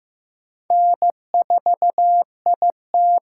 Morse code, heard as a keyed tone, about 15 wpm, 700 Hz. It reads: N4IT
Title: USA Call Signs - 5 Characters - 6x Character Spacing - 15wpm